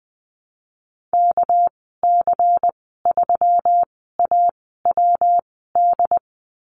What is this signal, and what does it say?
Morse code, heard as a keyed tone, about 20 wpm, spaced as for 14 wpm, 700 Hz. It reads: KC3AWD
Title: 200 Lesson 32g - Letter k - Call Signs 20wpm